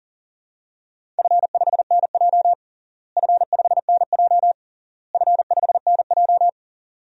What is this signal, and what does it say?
Morse code, heard as a keyed tone, about 40 wpm, 700 Hz. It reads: F5NJ F5NJ F5NJ
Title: DXCC-100 Call Signs - 4 Characters - Repeated 3x - 40wpm